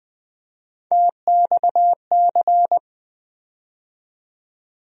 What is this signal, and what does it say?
Morse code, heard as a keyed tone, about 20 wpm, 700 Hz. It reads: TXC
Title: USA Call Sign Postfixes - 3 Letters 20wpm